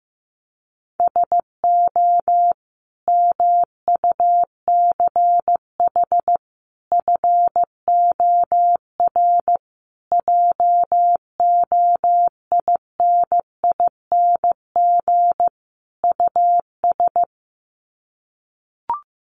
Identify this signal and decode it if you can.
Morse code, heard as a keyed tone, about 14 wpm, 700 Hz. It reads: SO MUCH FOR JOINING US
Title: Sets of 5 Words 15wpm